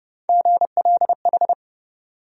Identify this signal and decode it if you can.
Morse code, heard as a keyed tone, about 32 wpm, 700 Hz. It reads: GLH
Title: USA Call Sign Postfixes - 3 Letters 30wpm